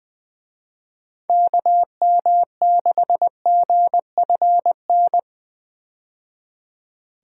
Morse code, heard as a keyed tone, about 20 wpm, 700 Hz. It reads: KM6GFN
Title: USA Call Signs - 6 Characters - 20wpm